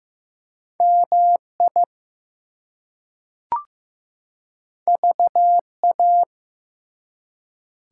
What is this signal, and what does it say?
Morse code, heard as a keyed tone, about 15 wpm, 700 Hz. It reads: MI VA
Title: US States and Canadian Provinces 15wpm